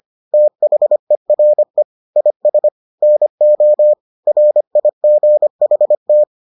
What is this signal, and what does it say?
Morse code, heard as a keyed tone, about 25 wpm, 600 Hz. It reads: THERE IS NO RIGHT